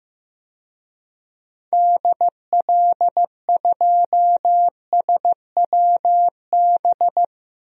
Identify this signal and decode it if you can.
Morse code, heard as a keyed tone, about 15 wpm, 700 Hz. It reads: DL2SWB
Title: DXCC-100 Call Signs - 6 Characters - 15wpm